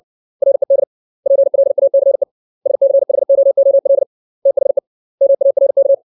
Morse code, heard as a keyed tone, about 60 wpm, 550 Hz. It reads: RED PLACE SCHOOL THE MARK
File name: 632 words / 632 Words 60WPM